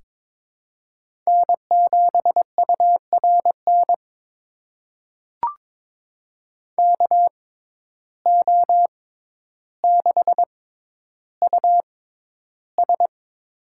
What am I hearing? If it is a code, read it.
Morse code, heard as a keyed tone, about 22 wpm, spaced as for 11 wpm, 700 Hz. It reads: N7URN KO6US
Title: USA Call Signs - 6x Character Spacing - 22wpm